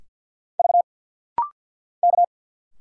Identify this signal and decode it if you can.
Morse code, heard as a keyed tone, about 50 wpm, 690 Hz. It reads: V K
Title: Single Letters - Rapid-Fire 50wpm